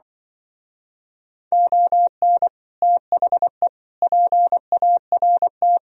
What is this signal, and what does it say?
Morse code, heard as a keyed tone, about 25 wpm, 700 Hz. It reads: ON THE PART
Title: Speed-Racing - Sets of 4 Words 20wpm